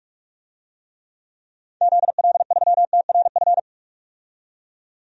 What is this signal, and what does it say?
Morse code, heard as a keyed tone, about 43 wpm, 700 Hz. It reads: ZL3TRF
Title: DXCC-100 Call Signs - 6 Characters - 45wpm